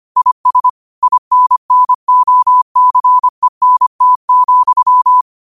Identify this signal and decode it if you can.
Morse code, heard as a keyed tone, about 25 wpm, 1000 Hz. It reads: IS INNOCENT,
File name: qotdfiles / 25wpm